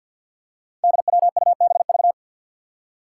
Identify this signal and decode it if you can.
Morse code, heard as a keyed tone, about 53 wpm, 700 Hz. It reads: DWUBV
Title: Five Challenging Letter-Number Combinations 50wpm